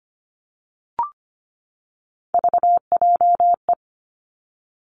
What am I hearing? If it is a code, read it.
Morse code, heard as a keyed tone, about 25 wpm, 700 Hz. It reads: VJE
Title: Three Challenging Letter-Number Combinations 25wpm